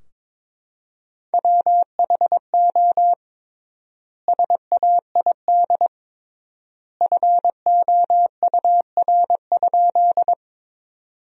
Morse code, heard as a keyed tone, about 22 wpm, 700 Hz. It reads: WHO SAID FOUR?
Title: 3x Word-Spacing - Sentences from Top 300 Words - 22wpm